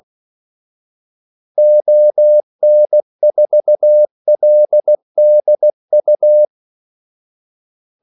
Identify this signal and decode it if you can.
Morse code, heard as a keyed tone, about 16 wpm, 600 Hz. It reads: ON4LDU
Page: Kaldesignaler 16 wpm